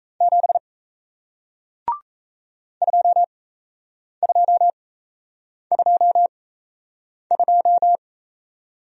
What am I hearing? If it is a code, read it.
Morse code, heard as a keyed tone, about 37 wpm, spaced as for 8 wpm, 700 Hz. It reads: 7 2222